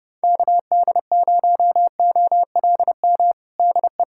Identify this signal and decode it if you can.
Morse code, heard as a keyed tone, about 30 wpm, 700 Hz. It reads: KD0OLM DE
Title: USA Callsign DE Callsign - 6 Characters - 30wpm